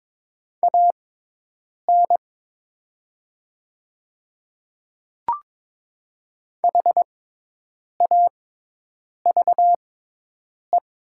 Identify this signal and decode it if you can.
Morse code, heard as a keyed tone, about 23 wpm, spaced as for 8 wpm, 700 Hz. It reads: AN HAVE
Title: 6x Character Spacing - Top 100 Words - 22wpm